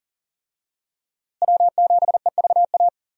Morse code, heard as a keyed tone, about 40 wpm, 700 Hz. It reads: W7EVA
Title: USA Call Signs - 5 Characters - Repeated 3x - 40wpm